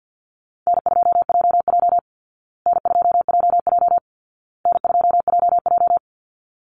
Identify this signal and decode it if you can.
Morse code, heard as a keyed tone, about 48 wpm, 700 Hz. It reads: N2JJ N2JJ N2JJ
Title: Easy Call Signs Repeated 3x - 50wpm